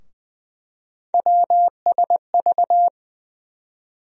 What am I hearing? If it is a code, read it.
Morse code, heard as a keyed tone, about 20 wpm, 700 Hz. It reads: WSV
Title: USA Call Sign Postfixes - 3 Letters 20wpm